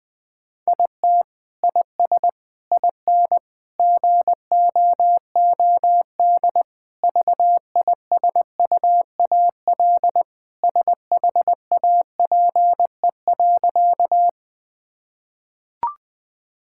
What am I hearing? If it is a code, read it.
Morse code, heard as a keyed tone, about 20 wpm, 700 Hz. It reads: IT IS IN GOOD VISUAL SHAPE.
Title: Sentences from Top 1000 Words - Review Part 6 - 20wpm